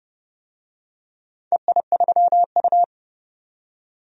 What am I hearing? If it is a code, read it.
Morse code, heard as a keyed tone, about 31 wpm, 700 Hz. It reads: EI3U
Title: DXCC-100 Call Signs - 4 Characters - 30wpm